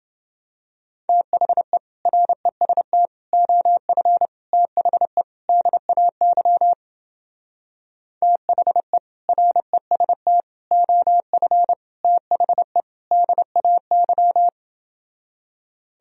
Morse code, heard as a keyed tone, about 28 wpm, 700 Hz. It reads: THE REST OF THE DAY THE REST OF THE DAY